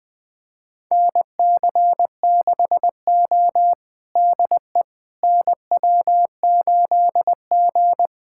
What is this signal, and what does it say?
Morse code, heard as a keyed tone, about 21 wpm, 700 Hz. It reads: NC6O DE NW8G
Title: USA Callsign DE Callsign - 4 Characters - 20wpm